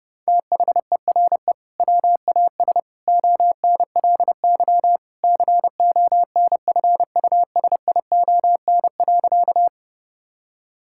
Morse code, heard as a keyed tone, about 30 wpm, 700 Hz. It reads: THERE WAS ONLY CONFUSION.
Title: Sentences from Top 1600 Words 30wpm